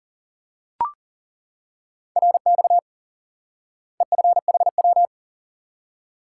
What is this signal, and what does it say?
Morse code, heard as a keyed tone, about 40 wpm, 700 Hz. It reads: RX EFHW
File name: IQE Course 3 - Lesson 89 - YAGI - Two Elements - 40wpm